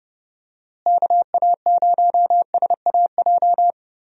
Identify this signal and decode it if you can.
Morse code, heard as a keyed tone, about 30 wpm, 700 Hz. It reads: KA0SAJ
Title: USA Call Signs - 6 Characters - Repeated 3x - 30wpm